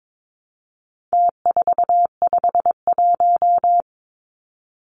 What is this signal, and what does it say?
Morse code, heard as a keyed tone, about 22 wpm, 700 Hz. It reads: T451
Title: Numbers - Contest Style - 4-Digit 22wpm